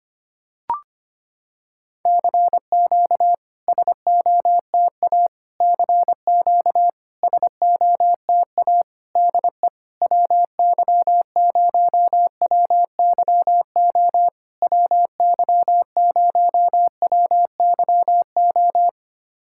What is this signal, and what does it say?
Morse code, heard as a keyed tone, about 25 wpm, 700 Hz. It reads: CQ SOTA CQ SOTA DE WY0WYO WY0WYO
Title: SOTA QSO Practice 25wpm